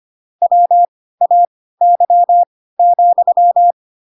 Morse code, heard as a keyed tone, about 25 wpm, spaced as for 16 wpm, 700 Hz. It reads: WAY,
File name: Pinocchio / Pinocchio ch2 10 25 ik1hge